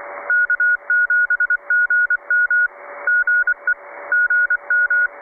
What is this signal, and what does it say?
Morse code, heard as a keyed tone, about 24 wpm, 1440 Hz. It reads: K7GM GE GM
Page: 1821 kHz (1822 LSB, 1,8kHzBandbreite, 8m Vertikal)